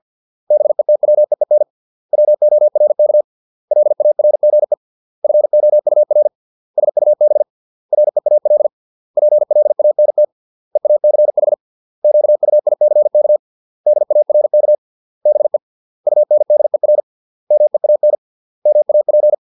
Morse code, heard as a keyed tone, about 50 wpm, 600 Hz. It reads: BETWEEN WORK LARGE FOUR SUB REAL PLANT EACH QUICK DARK BE UNDER MEAN MAP